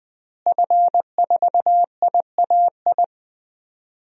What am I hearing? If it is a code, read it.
Morse code, heard as a keyed tone, about 20 wpm, 700 Hz. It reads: F4IAI
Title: DXCC-100 Call Signs - 5 Characters - Repeated 3x - 20wpm